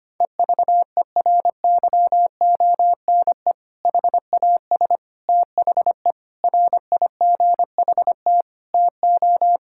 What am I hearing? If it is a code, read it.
Morse code, heard as a keyed tone, about 25 wpm, 700 Hz. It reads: EVERYONE HAS THE RIGHT TO
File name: practice / 25wpm / Sentences with 3-Word Phrases - 25wpm